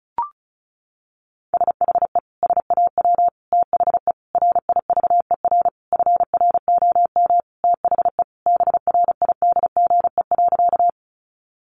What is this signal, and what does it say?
Morse code, heard as a keyed tone, about 34 wpm, 700 Hz. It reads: SHE SAW THE RIVER FROM THE BRIDGE.
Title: Sentences from Top 1200 Words 35wpm